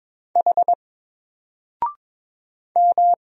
Morse code, heard as a keyed tone, about 22 wpm, 700 Hz. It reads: H M
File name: Single-Letter-Number-22wpm